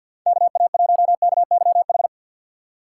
Morse code, heard as a keyed tone, about 53 wpm, 700 Hz. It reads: KA1KYH
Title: USA Call Signs - 6 Characters - 4x Character Spacing - 50wpm